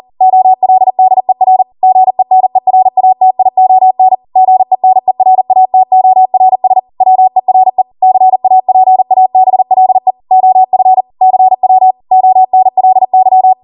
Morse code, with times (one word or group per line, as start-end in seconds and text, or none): 0.2-1.6 OLDER
1.8-4.1 GENERATION
4.4-6.8 GENERATORS
7.0-7.8 WERE
8.0-10.1 CAPABLE
10.3-11.0 OF
11.2-11.9 CW
12.1-13.5 ONLY